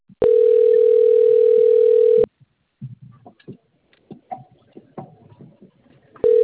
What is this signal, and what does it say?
Ringback tone (North America), 440 + 480 Hz.